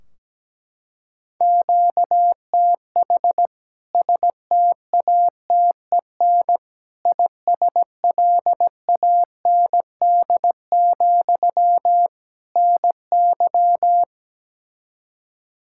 Morse code, heard as a keyed tone, about 17 wpm, 700 Hz. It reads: QTH STATEN ISLAND, NY
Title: K1USN SST QSO Practice 17wpm